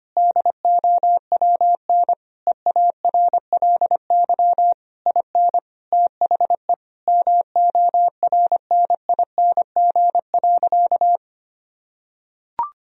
Morse code, heard as a keyed tone, about 25 wpm, 700 Hz. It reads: DOWN EARLY IN THE MORNING.
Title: Sentences from Top 1000 Words - Review Part 2 - 25wpm